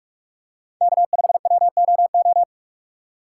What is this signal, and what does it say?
Morse code, heard as a keyed tone, about 44 wpm, 700 Hz. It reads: K5WOO